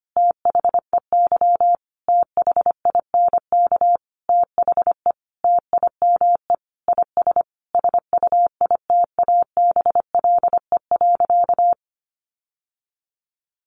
Morse code, heard as a keyed tone, about 25 wpm, 700 Hz. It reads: THEY THINK THE TIME IS SUITABLE.
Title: Sentences from Top 1400 Words 25wpm